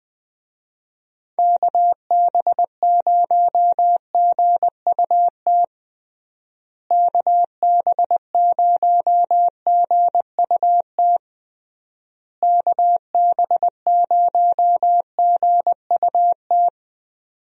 Morse code, heard as a keyed tone, about 20 wpm, 700 Hz. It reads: KB0GUT KB0GUT KB0GUT
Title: USA Call Signs - 6 Characters - Repeated 3x - 20wpm